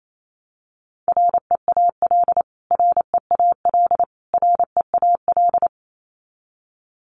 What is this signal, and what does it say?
Morse code, heard as a keyed tone, about 28 wpm, 700 Hz. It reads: REAL REAL REAL
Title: Top 500 Words Repeated 28wpm